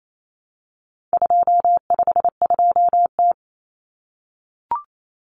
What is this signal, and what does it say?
Morse code, heard as a keyed tone, about 28 wpm, 700 Hz. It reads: 252T
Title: Common Cut Numbers - 4-Digit 28wpm